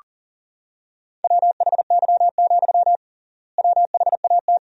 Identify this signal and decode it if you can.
Morse code, heard as a keyed tone, about 37 wpm, 700 Hz. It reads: WHY, WHAT
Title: 3x Word-Spacing - Sentences from Top 200 Words - 40wpm